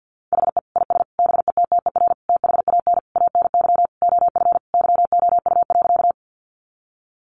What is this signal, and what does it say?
Morse code, heard as a keyed tone, about 49 wpm, 700 Hz. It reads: HE IS BETTER THAN ANY OF YOU.
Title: Sentences from Top 400 Words 50wpm